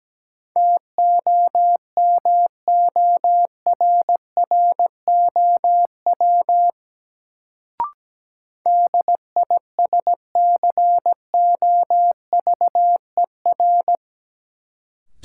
Morse code, heard as a keyed tone, about 17 wpm, 700 Hz. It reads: TOMORROW DISCOVER
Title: Sentences from Top 1100 Words - New Word Drill - 17wpm